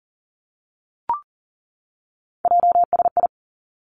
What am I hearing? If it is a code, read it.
Morse code, heard as a keyed tone, about 41 wpm, 700 Hz. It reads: JSI
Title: Three Challenging Letter-Number Combinations 40wpm